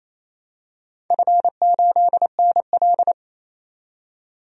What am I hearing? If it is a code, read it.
Morse code, heard as a keyed tone, about 28 wpm, 700 Hz. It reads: F8NL
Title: DXCC-100 Call Signs - 4 Characters - 28wpm